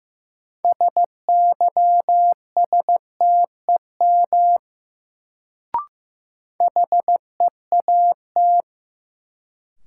Morse code, heard as a keyed tone, about 15 wpm, 700 Hz. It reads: SYSTEM HEAT
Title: Sentences from Top 1000 Words - Review Part 3 - New Word Drill - 15wpm